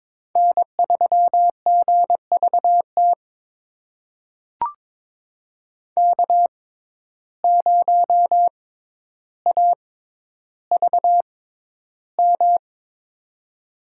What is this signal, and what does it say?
Morse code, heard as a keyed tone, about 22 wpm, spaced as for 11 wpm, 700 Hz. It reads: N3GVT K0AVM
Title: USA Call Signs - 5 Characters - 6x Character Spacing - 22wpm